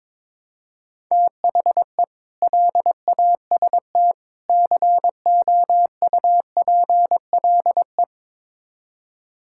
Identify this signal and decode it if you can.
Morse code, heard as a keyed tone, about 22 wpm, 700 Hz. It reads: THE LAST COUPLE
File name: practice / 22wpm / Build A Phrase - 5-Word Phrases - 22wpm